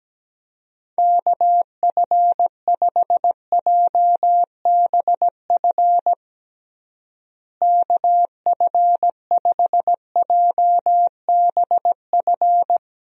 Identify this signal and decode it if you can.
Morse code, heard as a keyed tone, about 17 wpm, 700 Hz. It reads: KF5JBF KF5JBF
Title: USA Call Signs - 6 Characters - Repeated 2x - 17wpm